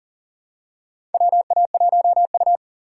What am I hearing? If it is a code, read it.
Morse code, heard as a keyed tone, about 40 wpm, 700 Hz. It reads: WA1U
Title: USA Call Signs - 4 Characters - 3x Character Spacing - 40wpm